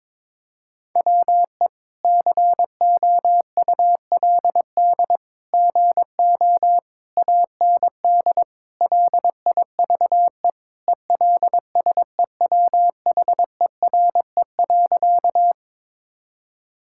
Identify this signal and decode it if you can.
Morse code, heard as a keyed tone, about 22 wpm, 700 Hz. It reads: WE COULD GO AND LIVE ELSEWHERE.
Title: Sentences from Top 1600 Words 22wpm